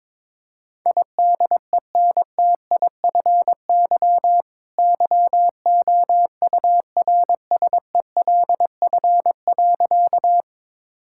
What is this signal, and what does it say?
Morse code, heard as a keyed tone, about 22 wpm, 700 Hz. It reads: IDENTIFY YOURSELF.
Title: Sentences from Top 1000 Words - Review Part 6 - 22wpm